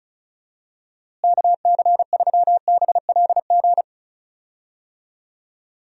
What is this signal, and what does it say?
Morse code, heard as a keyed tone, about 36 wpm, 700 Hz. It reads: KC3BLG
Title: USA Call Signs - 6 Characters - Repeated 3x - 35wpm